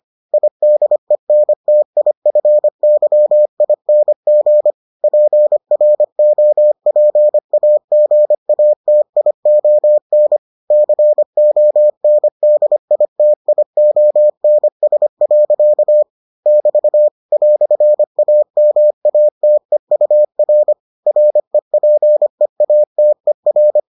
Morse code, heard as a keyed tone, about 25 wpm, 600 Hz. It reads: IDENTIFYING PROPAGATION CONDITIONS. <BT> "AMATEUR REPEATER